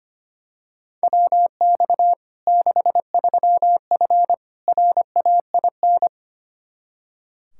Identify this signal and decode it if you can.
Morse code, heard as a keyed tone, about 25 wpm, 700 Hz. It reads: WX 63F RAIN
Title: Practice QSO - Exchange 2 - Info - 25wpm